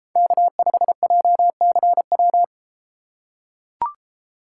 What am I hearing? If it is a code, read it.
Morse code, heard as a keyed tone, about 31 wpm, 700 Hz. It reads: K5JCW